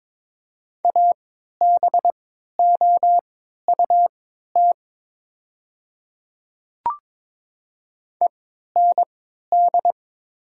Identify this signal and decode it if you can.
Morse code, heard as a keyed tone, about 22 wpm, spaced as for 13 wpm, 700 Hz. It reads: ABOUT END
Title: 3x Character Spacing - Top 500 Words - 22wpm